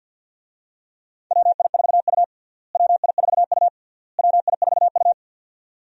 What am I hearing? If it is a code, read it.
Morse code, heard as a keyed tone, about 50 wpm, 700 Hz. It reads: WI4U WI4U WI4U
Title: USA Call Signs - 4 Characters - Repeated 3x - 50wpm